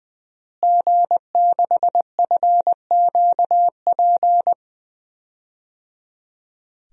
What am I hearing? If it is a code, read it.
Morse code, heard as a keyed tone, about 21 wpm, 700 Hz. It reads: G6FQP
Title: DXCC-100 Call Signs - 5 Characters - Repeated 3x - 20wpm